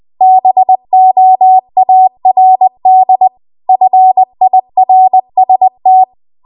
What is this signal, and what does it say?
Morse code, heard as a keyed tone, about 20 wpm, 750 Hz. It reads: BOARD FIRST